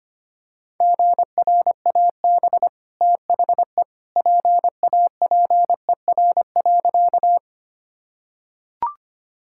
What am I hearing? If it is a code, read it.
Morse code, heard as a keyed tone, about 25 wpm, 700 Hz. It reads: GRAB THE PAPER.
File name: Easy Sentences with Hints - 25wpm